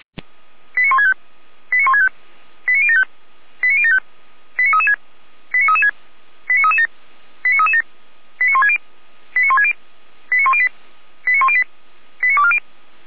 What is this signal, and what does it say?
Five-tone selective call (ZVEI), heard as quick runs of five tones, tones of about 80 ms. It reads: 89165 89165 89075 89075 89336 89336 89337 89337 89160 89160 89108 89108 89240
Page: /455/SDR/ZVEI/